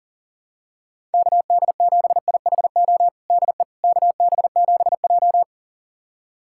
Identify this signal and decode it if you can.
Morse code, heard as a keyed tone, about 40 wpm, 700 Hz. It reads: KD7IHO DE KB7J
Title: USA Callsign DE Callsign 40wpm